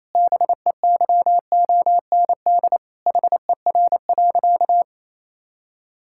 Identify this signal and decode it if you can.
Morse code, heard as a keyed tone, about 27 wpm, 700 Hz. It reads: BEYOND HER.
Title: Sentences from Top 1000 Words - Review Part 6 - 28wpm